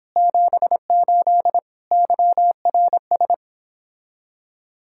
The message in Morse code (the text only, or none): 78 YRS